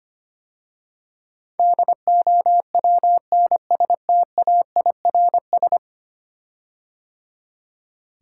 Morse code, heard as a keyed tone, about 25 wpm, 700 Hz. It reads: DOWNSTAIRS
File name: 10 Letter Words - First 3 Letters Given - 25wpm